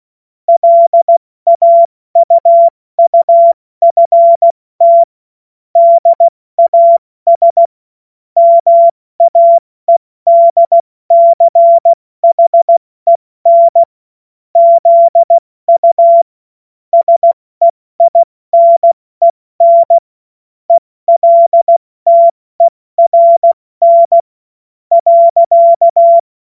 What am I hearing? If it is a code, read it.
Morse code, heard as a keyed tone, about 16 wpm, spaced as for 14 wpm, 680 Hz. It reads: LAUUFT DAS MAEDCHEN ZU SEINEN ELTERN.
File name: Morsekurs / Witze / Witz 25. April 70 F80